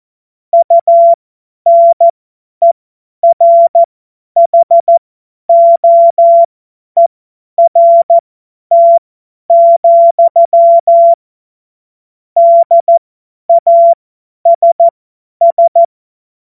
Morse code, heard as a keyed tone, about 14 wpm, spaced as for 10 wpm, 680 Hz. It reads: UNERHOERT, DASS
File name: Morsekurs / Witze / Witz 10. Mai 2022 50 F70